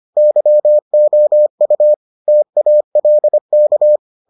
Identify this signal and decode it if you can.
Morse code, heard as a keyed tone, about 25 wpm, 600 Hz. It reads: YOU TALK